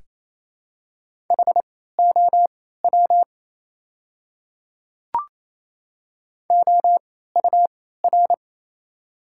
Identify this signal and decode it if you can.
Morse code, heard as a keyed tone, about 27 wpm, spaced as for 16 wpm, 700 Hz. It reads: HOW OUR
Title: 3x Character Spacing - Top 200 Words - 28wpm